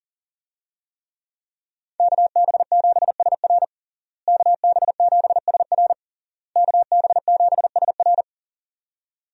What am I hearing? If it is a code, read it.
Morse code, heard as a keyed tone, about 37 wpm, 700 Hz. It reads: KB7SR KB7SR KB7SR
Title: USA Call Signs - 5 Characters - Repeated 3x - 40wpm